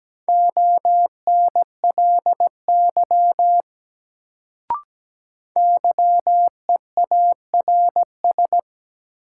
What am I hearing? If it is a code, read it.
Morse code, heard as a keyed tone, about 17 wpm, 700 Hz. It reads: ONLY YEARS
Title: Sets of 2 Words - New Word Drill - 17wpm